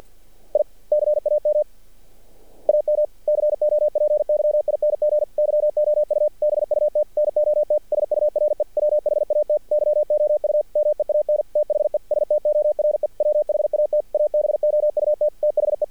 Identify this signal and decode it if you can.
Morse code, heard as a keyed tone, about 47 wpm, 600 Hz. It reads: I CAM AM COPYING YOU BUT NOT SURE WHAT YOU MEAN THE STORE WHAT ABOUT THE
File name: archived / CRSnet